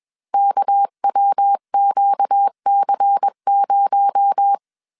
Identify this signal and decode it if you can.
Morse code, heard as a keyed tone, about 21 wpm, 800 Hz. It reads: KWQC0
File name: CW / DET / 1 9 1500C S C 20wpm 2310